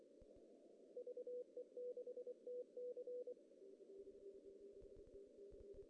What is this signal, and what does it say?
Morse code, heard as a keyed tone, about 24 wpm, 490 Hz. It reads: VE6TC
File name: CW / cqww-cw-k5tr